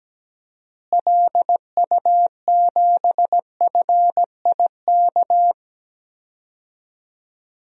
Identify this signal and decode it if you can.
Morse code, heard as a keyed tone, about 17 wpm, 700 Hz. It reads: LU7FIK
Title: DXCC-100 Call Signs - 6 Characters - Repeated 2x - 17wpm